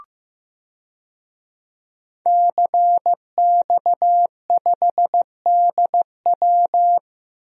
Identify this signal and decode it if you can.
Morse code, heard as a keyed tone, about 15 wpm, 700 Hz. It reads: CX5DW
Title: DXCC-100 Call Signs - 5 Characters - Repeated 3x - 15wpm